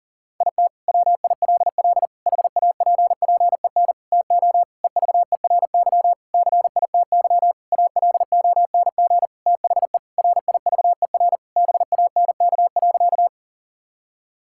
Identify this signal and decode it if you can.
Morse code, heard as a keyed tone, about 40 wpm, 700 Hz. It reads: IT WILL HAPPEN TO EVERY CITY ALONG THE RIVER BANK.
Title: Sentences from Top 1000 Words - Review Part 6 - 40wpm